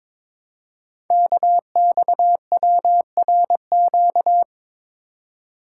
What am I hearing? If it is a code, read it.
Morse code, heard as a keyed tone, about 22 wpm, 700 Hz. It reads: KXWRQ
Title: Code Groups - 5 Letters - 22wpm